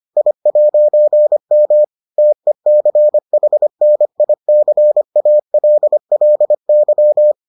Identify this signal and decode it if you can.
Morse code, heard as a keyed tone, about 25 wpm, 600 Hz. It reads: I'M TECHNICALLY